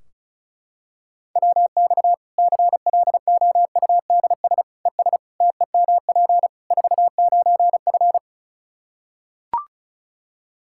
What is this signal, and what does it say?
Morse code, heard as a keyed tone, about 34 wpm, 700 Hz. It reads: WX CLOUDS ES TEMP 49F
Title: K1USN SST QSO Practice 35wpm